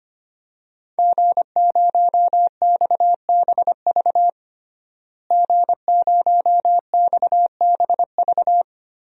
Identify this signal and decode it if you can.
Morse code, heard as a keyed tone, about 25 wpm, 700 Hz. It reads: G0XBV G0XBV
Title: DXCC-100 Call Signs - 5 Characters - Repeated 3x - 25wpm